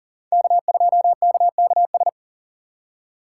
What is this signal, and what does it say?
Morse code, heard as a keyed tone, about 51 wpm, 700 Hz. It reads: K2KKS